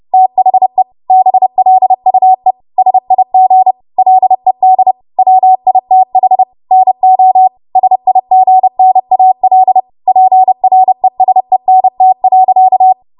Morse code, heard as a keyed tone, about 30 wpm, 750 Hz. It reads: THE BLUE SIG LED WITH NO SIGNAL PRESENT.
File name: Archive / 30 WPM / 30WPM